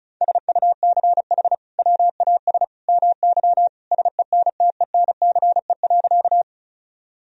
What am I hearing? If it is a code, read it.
Morse code, heard as a keyed tone, about 36 wpm, 700 Hz. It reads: SUCH WAS MY SENTENCE.
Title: Sentences from Top 1000 Words - Review Part 1 - 35wpm